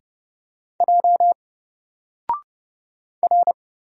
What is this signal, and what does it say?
Morse code, heard as a keyed tone, about 29 wpm, 700 Hz. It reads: J R
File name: Single-Letter-Number-30wpm